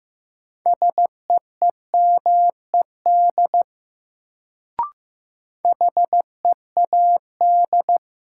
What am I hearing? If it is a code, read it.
Morse code, heard as a keyed tone, about 15 wpm, 700 Hz. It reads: SEEMED HEAD
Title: Sets of 3 Words - New Word Drill - 15wpm